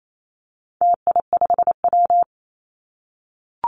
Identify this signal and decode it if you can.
Morse code, heard as a keyed tone, about 28 wpm, 700 Hz. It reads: TI5W
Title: Call Signs - 28wpm